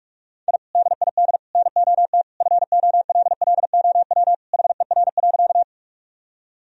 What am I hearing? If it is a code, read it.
Morse code, heard as a keyed tone, about 43 wpm, 700 Hz. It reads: I DID NOT FOLLOW HER.
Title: Sentences from Top 1000 Words - Review Part 1 - 45wpm